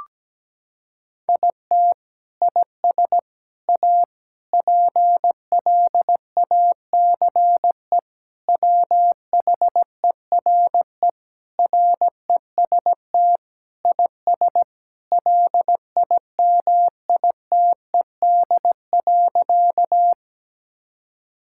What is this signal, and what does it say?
Morse code, heard as a keyed tone, about 17 wpm, 700 Hz. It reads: IT IS A PLACE WHERE REST IS LIMITED.
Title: Sentences from Top 1000 Words - Review Part 6 - 17wpm